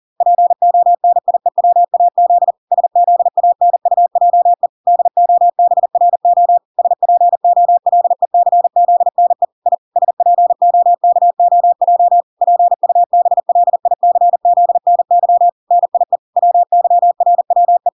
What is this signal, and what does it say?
Morse code, heard as a keyed tone, about 40 wpm, 700 Hz. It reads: PONIEWAZ SZANUJE DOBRO SPOLECZNE I SPOKOJ PUBLICZNY NIE WYRWE